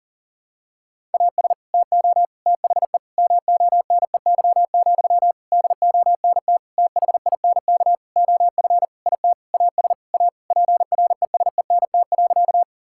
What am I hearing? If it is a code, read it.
Morse code, heard as a keyed tone, about 40 wpm, 700 Hz. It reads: AS TO THE MONEY, DONT THINK OF IT AS A PRESENT.
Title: Sentences from Top 1000 Words - Review Part 5 - 40wpm